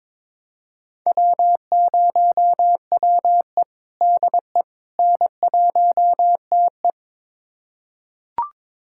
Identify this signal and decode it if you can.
Morse code, heard as a keyed tone, about 22 wpm, 700 Hz. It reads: W0WE DE N1TE
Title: USA Callsign DE Callsign - 4 Characters - 22wpm